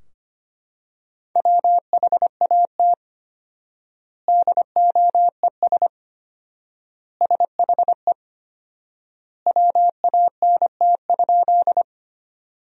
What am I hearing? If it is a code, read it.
Morse code, heard as a keyed tone, about 25 wpm, 700 Hz. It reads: WHAT DOES SHE WANT?
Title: 4x Word-Spacing - Sentences from Top 200 Words - 25wpm